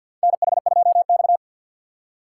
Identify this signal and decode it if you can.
Morse code, heard as a keyed tone, about 52 wpm, 700 Hz. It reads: NH2X
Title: USA Call Signs - 4 Characters - 2x Character Spacing - 50wpm